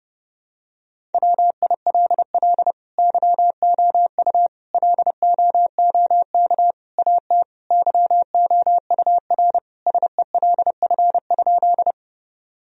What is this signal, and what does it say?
Morse code, heard as a keyed tone, about 30 wpm, 700 Hz. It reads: WILL YOU LOOK AT YOUR SELF?
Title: Sentences from Top 1000 Words - Review Part 1 - 30wpm